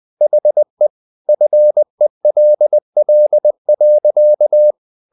Morse code, heard as a keyed tone, about 20 wpm, 600 Hz. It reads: HE FELL.
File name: the old man and the sea part 2 20wpm